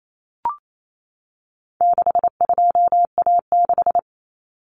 Morse code, heard as a keyed tone, about 28 wpm, 700 Hz. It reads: 62A6
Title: Common Cut Numbers - 4-Digit 28wpm